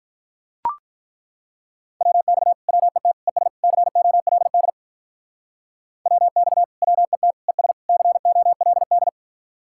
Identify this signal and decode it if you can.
Morse code, heard as a keyed tone, about 42 wpm, 700 Hz. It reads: WX WET ES COLD WX WET ES COLD
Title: Speed-Racing Common QSO Phrases - 35wpm